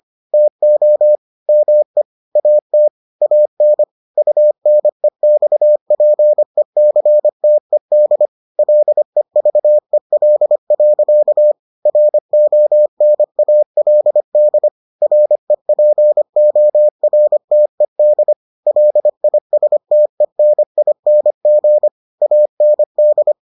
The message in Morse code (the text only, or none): TO ME AT AN UNEXPECTED LEVEL. RONALD REPORTED LISTENING AND